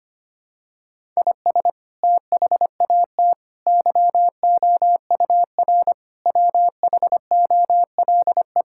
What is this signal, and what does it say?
Morse code, heard as a keyed tone, about 25 wpm, 700 Hz. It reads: IS THAT YOUR WHOLE
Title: Sentences from Top 1000 Words - Review Part 5 - 25wpm